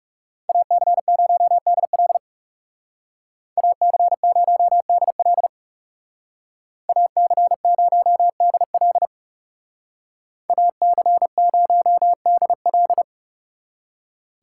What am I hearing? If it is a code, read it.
Morse code, heard as a keyed tone, about 39 wpm, 700 Hz. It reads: AC0DL AC0DL AC0DL AC0DL